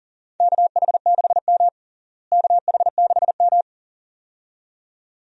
Morse code, heard as a keyed tone, about 37 wpm, 700 Hz. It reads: KH6M KH6M
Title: USA Call Signs - 4 Characters - Repeated 2x - 40wpm